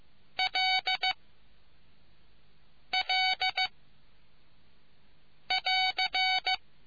Morse code, heard as a keyed tone, about 15 wpm, 750 Hz. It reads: L L <AR>